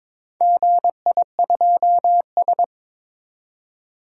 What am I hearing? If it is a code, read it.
Morse code, heard as a keyed tone, about 21 wpm, 700 Hz. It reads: GI2S
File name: practice / 22wpm / Four Letter-Number Combinations - 22wpm